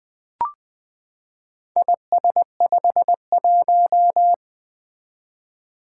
Morse code, heard as a keyed tone, about 20 wpm, 700 Hz. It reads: IS51
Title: Four Letter-Number Combinations - 20wpm